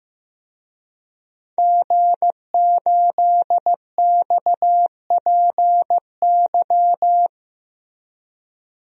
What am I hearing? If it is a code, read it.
Morse code, heard as a keyed tone, about 15 wpm, 700 Hz. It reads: G8XPY
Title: DXCC-100 Call Signs - 5 Characters - Repeated 3x - 15wpm